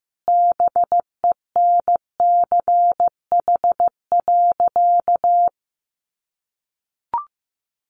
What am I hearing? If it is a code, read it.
Morse code, heard as a keyed tone, about 14 wpm, 700 Hz. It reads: BENCH.
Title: Sentences from Top 1400 Words 15wpm